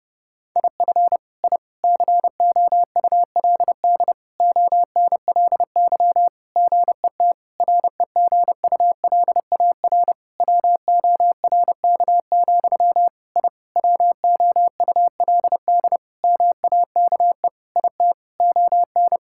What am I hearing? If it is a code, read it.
Morse code, heard as a keyed tone, about 32 wpm, 700 Hz. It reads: IF I COULD ONLY GET REGULAR WORK, I WOULD MAKE IT ON
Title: Sentences from Top 1000 Words - Review Part 6 - 30wpm